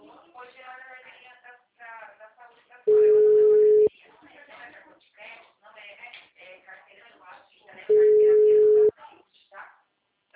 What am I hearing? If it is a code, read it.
Ringback tone (Europe, 425 Hz), with speech.
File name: out- -2023- - - .74999